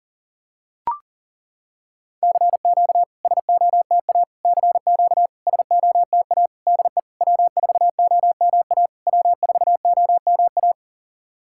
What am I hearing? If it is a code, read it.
Morse code, heard as a keyed tone, about 40 wpm, 700 Hz. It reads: CQ SOTA CQ SOTA DE W4OMA W4OMA
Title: SOTA QSO Practice 40wpm